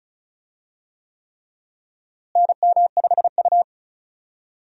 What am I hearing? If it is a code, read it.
Morse code, heard as a keyed tone, about 36 wpm, 700 Hz. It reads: NM5U
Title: USA Call Signs - 4 Characters - Repeated 3x - 35wpm